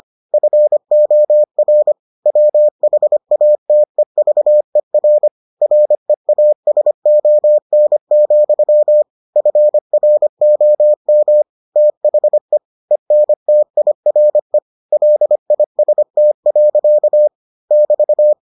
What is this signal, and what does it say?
Morse code, heard as a keyed tone, about 25 wpm, 600 Hz. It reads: FOR WHATEVER REASON, FROM THE ENTIRE LIST. <BT>